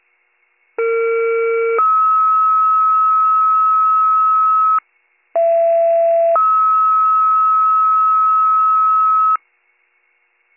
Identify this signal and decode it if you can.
Two-tone page, heard as a one-second tone, then a three-second tone, about 460 then 1290 Hz, about 670 then 1290 Hz. It is heard from 0.8 to 9.4 s.